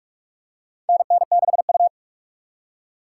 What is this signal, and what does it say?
Morse code, heard as a keyed tone, about 46 wpm, 700 Hz. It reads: NN6U
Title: USA Call Signs - 2x Character Spacing - 45wpm